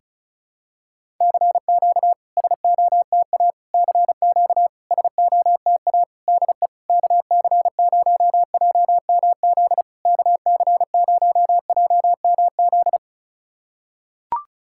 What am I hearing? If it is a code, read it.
Morse code, heard as a keyed tone, about 36 wpm, 700 Hz. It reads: CQ SOTA CQ SOTA DE KC0JMZ KC0JMZ
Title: SOTA QSO Practice 35wpm